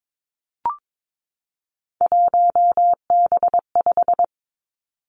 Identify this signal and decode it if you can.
Morse code, heard as a keyed tone, about 22 wpm, 700 Hz. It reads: 1B5
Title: Three Challenging Letter-Number Combinations 22wpm